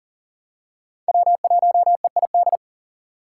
Morse code, heard as a keyed tone, about 40 wpm, 700 Hz. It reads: W1EID
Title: USA Call Signs - 6x Character Spacing - 40wpm